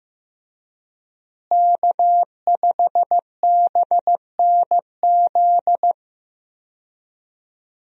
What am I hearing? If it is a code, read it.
Morse code, heard as a keyed tone, about 15 wpm, 700 Hz. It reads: K5BNZ
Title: USA Call Signs - 5 Characters - Repeated 2x - 15wpm